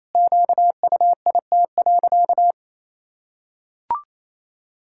Morse code, heard as a keyed tone, about 28 wpm, 700 Hz. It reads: QUIT.
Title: Pangram Challenge v2 - 28wpm